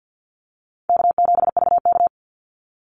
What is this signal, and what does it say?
Morse code, heard as a keyed tone, about 48 wpm, 700 Hz. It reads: K7VK
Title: NA SOTA Activator Callsigns 50wpm